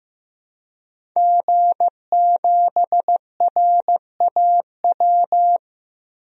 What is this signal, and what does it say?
Morse code, heard as a keyed tone, about 15 wpm, 700 Hz. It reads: G7RAW
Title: DXCC-100 Call Signs - 5 Characters - Repeated 3x - 15wpm